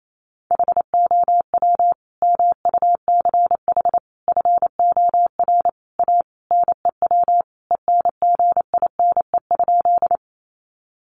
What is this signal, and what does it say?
Morse code, heard as a keyed tone, about 28 wpm, 700 Hz. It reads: HOW MUCH FOR A NEW ENGINE?
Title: Sentences from Top 1500 Words 28wpm